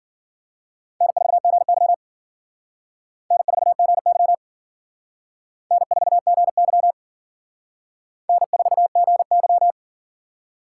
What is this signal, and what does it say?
Morse code, heard as a keyed tone, about 50 wpm, 700 Hz. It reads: N4GY N4GY N4GY N4GY
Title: Speed Racing - USA 1x2 Call Signs 40wpm